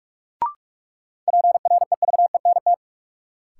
Morse code, heard as a keyed tone, about 46 wpm, 700 Hz. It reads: PREVENT
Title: Sentences from Top 1100 Words - New Word Drill - 45wpm